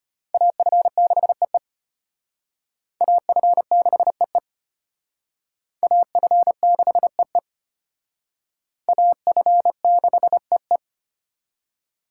Morse code, heard as a keyed tone, about 31 wpm, 700 Hz. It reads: AF6EE AF6EE AF6EE AF6EE
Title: Speed Racing - USA 2x2 Call Signs 25wpm